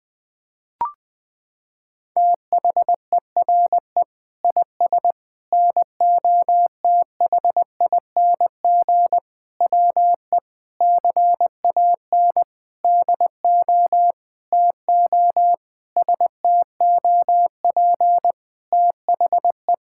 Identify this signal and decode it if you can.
Morse code, heard as a keyed tone, about 20 wpm, 700 Hz. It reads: THERE IS NOTHING WE CAN DO TO STOP THE
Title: Sentences from Top 1000 Words - Review Part 6 - 20wpm